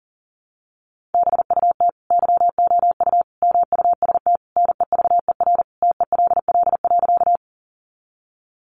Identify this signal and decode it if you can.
Morse code, heard as a keyed tone, about 40 wpm, 700 Hz. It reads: BUT YOU MUST NEVER TELL.
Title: Sentences from Top 300 Words 40wpm